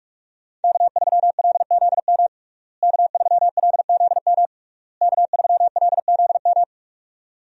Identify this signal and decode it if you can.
Morse code, heard as a keyed tone, about 46 wpm, 700 Hz. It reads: K3LZM K3LZM K3LZM
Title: USA Call Signs - 5 Characters - Repeated 3x - 45wpm